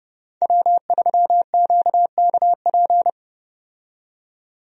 Morse code, heard as a keyed tone, about 30 wpm, 700 Hz. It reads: W3QKP
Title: USA Call Signs - 5 Characters - 30wpm